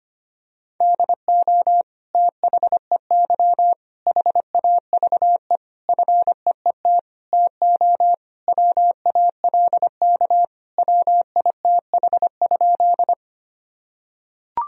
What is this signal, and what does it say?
Morse code, heard as a keyed tone, about 25 wpm, 700 Hz. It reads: DO THEY HAVE FEET TO WALK WITH?
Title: Sentences from Top 1000 Words - Review Part 1 - 25wpm